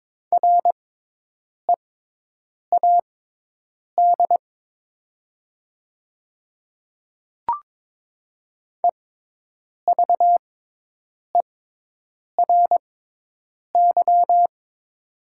Morse code, heard as a keyed tone, about 22 wpm, spaced as for 8 wpm, 700 Hz. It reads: READ EVERY
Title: 6x Character Spacing - Top 200 Words - 22wpm